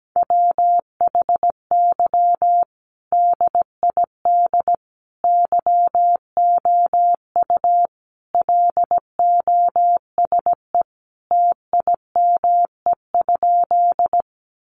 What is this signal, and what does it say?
Morse code, heard as a keyed tone, about 17 wpm, 700 Hz. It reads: WHY DID YOU LOSE TIME?
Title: Sentences from Top 1100 Words 17wpm